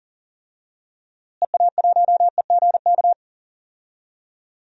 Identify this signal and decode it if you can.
Morse code, heard as a keyed tone, about 40 wpm, 700 Hz. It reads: EA1EGK
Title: DXCC-100 Call Signs - 6 Characters - Repeated 2x - 40wpm